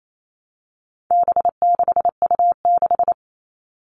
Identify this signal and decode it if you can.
Morse code, heard as a keyed tone, about 28 wpm, 700 Hz. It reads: B6U6
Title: All Cut Numbers - 4-Digit 28wpm